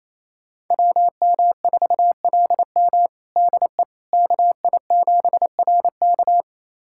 Morse code, heard as a keyed tone, about 28 wpm, 700 Hz. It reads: WM4LM DE KI7RK
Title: USA Callsign DE Callsign - 5 Characters - 28wpm